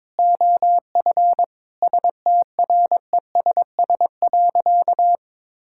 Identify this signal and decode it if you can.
Morse code, heard as a keyed tone, about 22 wpm, 700 Hz. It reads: OF STRESS.
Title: Sentences from Top 1000 Words - Review Part 6 - 22wpm